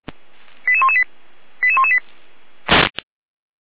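Five-tone selective call (ZVEI), heard as quick runs of five tones, tones of about 70 ms. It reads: 88118 88118
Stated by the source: ZVEI five-tone code 89319 received by SDR, from the feuerwehr-orlamuende.org (fire brigade) audio server.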